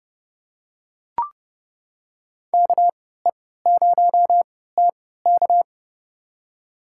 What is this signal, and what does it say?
Morse code, heard as a keyed tone, about 32 wpm, spaced as for 17 wpm, 700 Hz. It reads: KE0TK
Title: USA Call Signs - 5 Characters - 3x Character Spacing - 30wpm